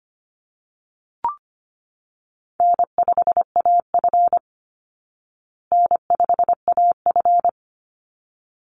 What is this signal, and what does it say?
Morse code, heard as a keyed tone, about 25 wpm, 700 Hz. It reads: N5AF N5AF
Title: Easy Call Signs Repeated 2x - 25wpm